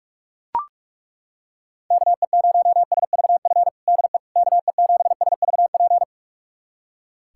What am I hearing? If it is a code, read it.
Morse code, heard as a keyed tone, about 42 wpm, 700 Hz. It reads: KE0SVF DE KE7SVP
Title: USA Callsign DE Callsign - 6 Characters - 45wpm